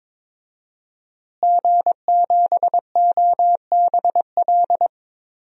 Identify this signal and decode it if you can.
Morse code, heard as a keyed tone, about 23 wpm, 700 Hz. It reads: G7OBL
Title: DXCC-100 Call Signs - 5 Characters - Repeated 3x - 22wpm